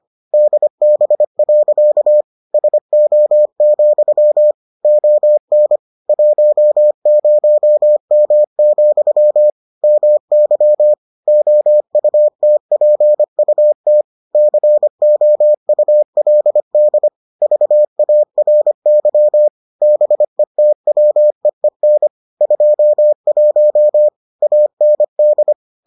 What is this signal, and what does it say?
Morse code, heard as a keyed tone, about 25 wpm, 600 Hz. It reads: DB. SO, ON 10M, MY OUTPUT COULD VARY BETWEEN 21 AND